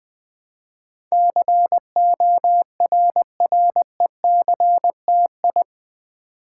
Morse code, heard as a keyed tone, about 20 wpm, 700 Hz. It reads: CORRECTI
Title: Build A Word - 4 Letters At A Time - 10-Letter Words - 20wpm